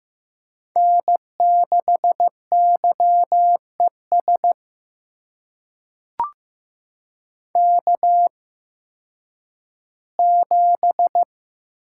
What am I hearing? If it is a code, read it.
Morse code, heard as a keyed tone, about 15 wpm, spaced as for 10 wpm, 700 Hz. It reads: N6YES K7
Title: USA Call Signs - 8x Character Spacing - 15wpm